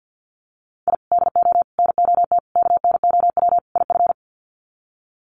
Morse code, heard as a keyed tone, about 50 wpm, 700 Hz. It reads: I DO NOT KNOW IF
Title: Sets of 5 Words 50wpm